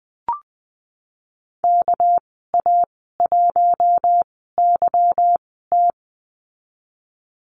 Morse code, heard as a keyed tone, about 20 wpm, spaced as for 15 wpm, 700 Hz. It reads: KA1YT
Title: 213 Lesson 34g - Number 8 - Call Signs 20wpm